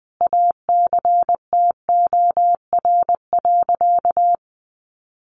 Morse code, heard as a keyed tone, about 20 wpm, 700 Hz. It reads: ACTOR.
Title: Sentences from Top 1500 Words 20wpm